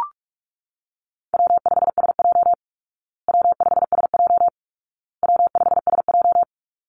Morse code, heard as a keyed tone, about 44 wpm, 700 Hz. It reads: W5SJ W5SJ W5SJ
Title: Call Signs Repeated 2x - 45wpm